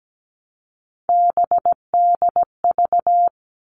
Morse code, heard as a keyed tone, about 17 wpm, 700 Hz. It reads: BDV
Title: All Cut Numbers - 3-Digit 17wpm